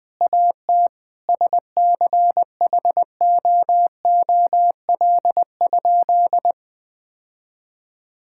Morse code, heard as a keyed tone, about 21 wpm, 700 Hz. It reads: AT SCHOOL?
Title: Sentences from Top 1000 Words - Review Part 6 - 20wpm